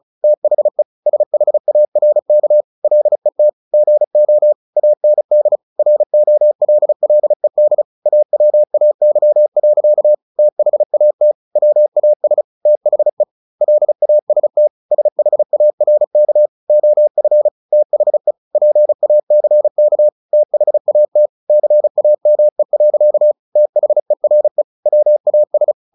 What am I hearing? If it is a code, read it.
Morse code, heard as a keyed tone, about 36 wpm, 600 Hz. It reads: THE SHARK LET GO AND ROLLED AWAY. THAT WAS THE LAST SHARK OF THE PACK THAT CAME. THERE WAS
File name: the old man and the sea 2 35wpm